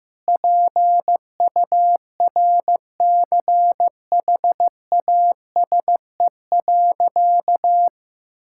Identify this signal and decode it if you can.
Morse code, heard as a keyed tone, about 16 wpm, 700 Hz. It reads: PURCHASE.
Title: Sentences from Top 1000 Words - Review Part 5 - 15wpm